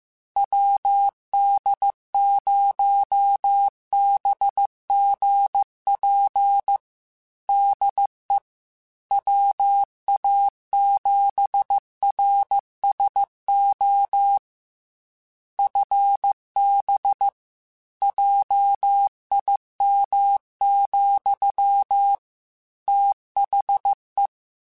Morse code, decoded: WD0BGP DE WA7RSO FB JIM, THE

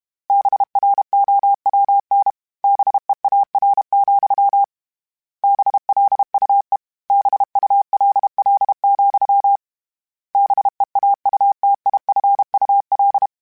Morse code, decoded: BROWN BEAR, BLUE BULL, BEAUTIFUL

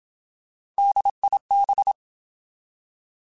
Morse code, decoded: DIB

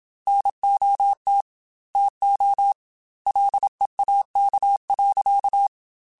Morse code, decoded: NOT TO LEAK.

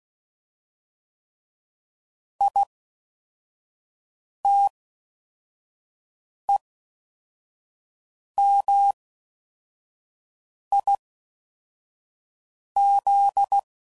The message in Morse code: ITEMIZ

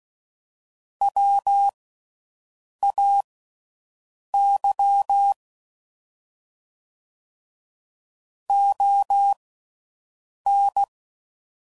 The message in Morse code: WAY ON

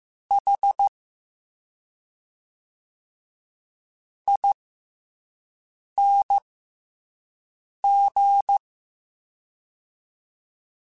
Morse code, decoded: H ING

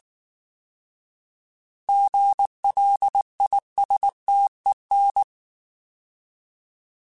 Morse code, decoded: GLISTEN

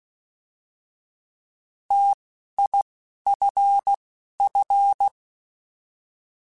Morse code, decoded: TIFF

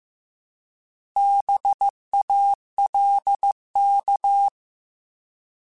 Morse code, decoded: BALK

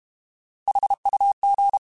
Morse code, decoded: HUG

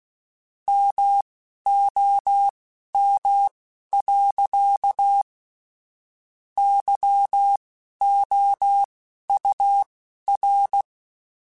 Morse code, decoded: MOM. YOUR

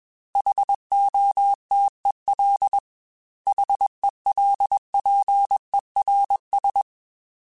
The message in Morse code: HOTEL HELPERS

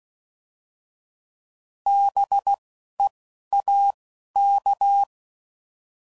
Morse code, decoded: B E A K

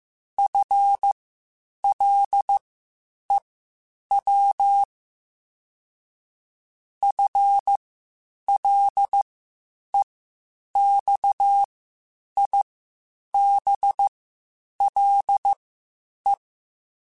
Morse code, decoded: FLEW FLEXIBLE

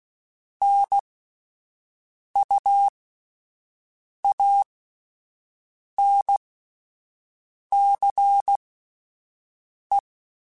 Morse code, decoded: NUANCE

